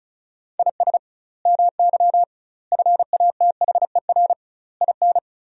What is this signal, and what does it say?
Morse code, heard as a keyed tone, about 36 wpm, 700 Hz. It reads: IS MY FATHER IN
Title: 2x Word-Spacing - Sentences from Top 500 Words - 35wpm